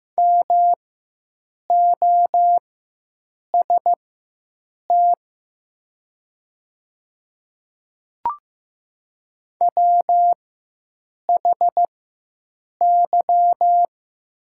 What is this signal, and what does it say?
Morse code, heard as a keyed tone, about 15 wpm, spaced as for 7 wpm, 700 Hz. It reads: MOST WHY
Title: 4x Character Spacing - Top 200 Words - 15wpm